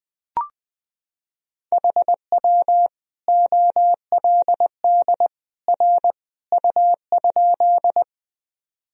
Morse code, heard as a keyed tone, about 20 wpm, 700 Hz. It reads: HW OLD R U?
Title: Sentences from Top 200 Words and Abbreviations 20wpm